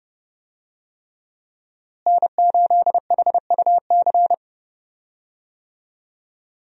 Morse code, heard as a keyed tone, about 30 wpm, 700 Hz. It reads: N8HUC
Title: USA Call Signs - 5 Characters - Repeated 3x - 30wpm